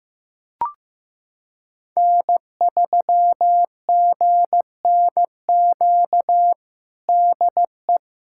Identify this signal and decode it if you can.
Morse code, heard as a keyed tone, about 16 wpm, 700 Hz. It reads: N3GNQ DE
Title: USA Callsign DE Callsign - 5 Characters - 15wpm